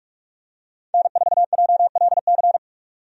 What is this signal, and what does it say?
Morse code, heard as a keyed tone, about 44 wpm, 700 Hz. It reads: N4JLC
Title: USA Call Signs - 5 Characters - 45wpm